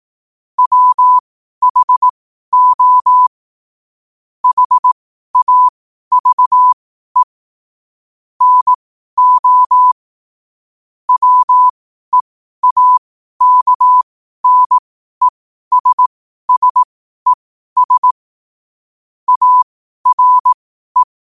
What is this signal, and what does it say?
Morse code, heard as a keyed tone, about 18 wpm, spaced as for 12 wpm, 1000 Hz. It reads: WHO HAVE NO WEAKNESSES ARE